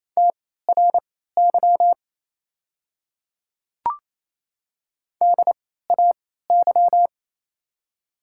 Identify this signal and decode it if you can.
Morse code, heard as a keyed tone, about 28 wpm, spaced as for 16 wpm, 700 Hz. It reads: TRY DAY
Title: 3x Character Spacing - Top 200 Words - 28wpm